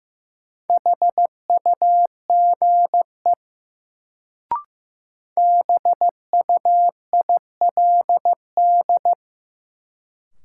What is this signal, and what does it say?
Morse code, heard as a keyed tone, about 15 wpm, 700 Hz. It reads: HUGE BUILD
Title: Sentences from Top 1100 Words - New Word Drill Alt - 15wpm